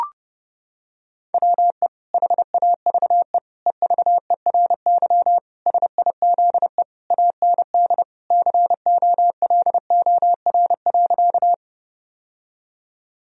Morse code, heard as a keyed tone, about 32 wpm, 700 Hz. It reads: WE HAVE EVERY SIZE AND COLOR.
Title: Sentences from Top 1000 Words - Review Part 2 - 30wpm